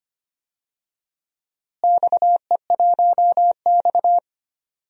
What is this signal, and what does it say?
Morse code, heard as a keyed tone, about 25 wpm, 700 Hz. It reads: XE1X
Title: DXCC-100 Call Signs - 4 Characters - Repeated 3x - 25wpm